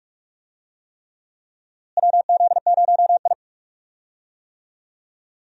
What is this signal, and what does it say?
Morse code, heard as a keyed tone, about 46 wpm, 700 Hz. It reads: WZ0I
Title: USA Call Signs - 4 Characters - Repeated 3x - 45wpm